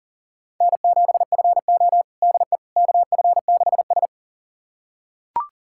Morse code, heard as a keyed tone, about 37 wpm, 700 Hz. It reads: N7FO DE KF6S
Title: USA Callsign DE Callsign - 4 Characters - 40wpm